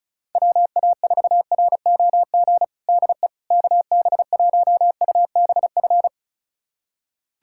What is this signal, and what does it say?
Morse code, heard as a keyed tone, about 34 wpm, 700 Hz. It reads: WA4ROG DE KB1UBF
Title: USA Callsign DE Callsign - 6 Characters - 35wpm